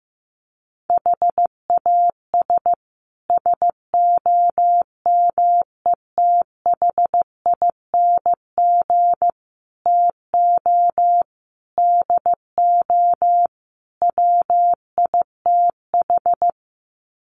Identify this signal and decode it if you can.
Morse code, heard as a keyed tone, about 15 wpm, 700 Hz. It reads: HAS SOMETHING TO DO WITH
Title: Sets of 5 Words Encore 15wpm